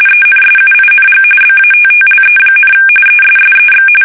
RTTY, about 45.45 baud. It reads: ENSUING BATTLE, ESSEX
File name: fsk850 / usnavy2